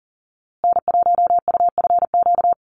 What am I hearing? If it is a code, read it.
Morse code, heard as a keyed tone, about 40 wpm, 700 Hz. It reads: N1UFQ
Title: USA Call Signs Repeated 3x 40wpm